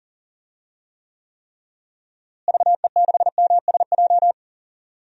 Morse code, heard as a keyed tone, about 40 wpm, 700 Hz. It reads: VE6MSJ
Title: DXCC-100 Call Signs - 6 Characters - Repeated 3x - 40wpm